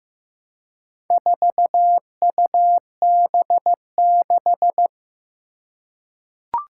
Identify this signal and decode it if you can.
Morse code, heard as a keyed tone, about 15 wpm, 700 Hz. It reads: 4UB6
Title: Four Challenging Letter-Number Combinations 15wpm